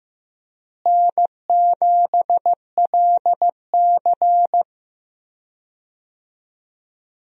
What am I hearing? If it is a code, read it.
Morse code, heard as a keyed tone, about 15 wpm, 700 Hz. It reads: N7LC
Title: USA Call Signs - 4 Characters - Repeated 2x - 15wpm